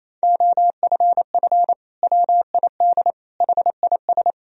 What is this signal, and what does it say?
Morse code, heard as a keyed tone, about 28 wpm, 700 Hz. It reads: OFF WID HIS
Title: Sentences from Top 300 Words and Abbreviations 28wpm